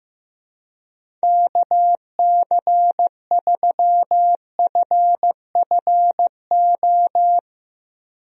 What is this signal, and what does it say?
Morse code, heard as a keyed tone, about 16 wpm, 700 Hz. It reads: KC3FFO
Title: USA Call Signs - 6 Characters - Repeated 3x - 15wpm